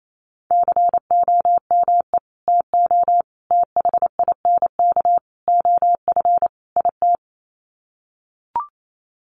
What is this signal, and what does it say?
Morse code, heard as a keyed tone, about 28 wpm, 690 Hz. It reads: COME TO THINK OF IT
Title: Sets of 5 Words Encore 28wpm